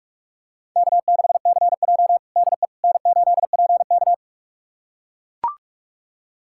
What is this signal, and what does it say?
Morse code, heard as a keyed tone, about 42 wpm, 700 Hz. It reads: K6CJ DE N8PK
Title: USA Callsign DE Callsign - 4 Characters - 45wpm